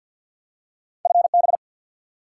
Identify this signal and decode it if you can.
Morse code, heard as a keyed tone, about 53 wpm, 700 Hz. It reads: FB
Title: Top 90 QSO Elements 50wpm